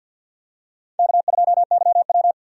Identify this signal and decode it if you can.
Morse code, heard as a keyed tone, about 46 wpm, 700 Hz. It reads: K2YW
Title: USA Call Signs - 4 Characters - 50wpm